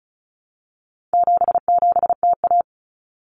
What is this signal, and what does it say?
Morse code, heard as a keyed tone, about 34 wpm, 700 Hz. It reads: 77TA
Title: Common Cut Numbers - 4-Digit 35wpm